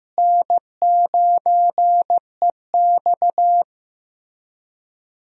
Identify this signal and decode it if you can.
Morse code, heard as a keyed tone, about 15 wpm, 700 Hz. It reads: N9EX
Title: USA Call Signs - 4 Characters - Repeated 3x - 15wpm